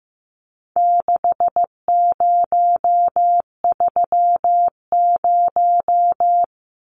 Morse code, heard as a keyed tone, about 15 wpm, 700 Hz. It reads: 6030
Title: Numbers - 4-Digit 15wpm